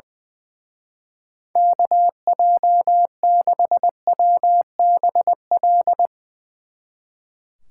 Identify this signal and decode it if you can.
Morse code, heard as a keyed tone, about 20 wpm, 700 Hz. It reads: KJ6WBL
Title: Speed Racing - USA Call Signs 20wpm